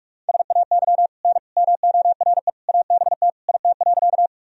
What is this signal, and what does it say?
Morse code, heard as a keyed tone, about 43 wpm, 700 Hz. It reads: SAY N MORE ABT IT.
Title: Sentences from Top 200 Words and Abbreviations 45wpm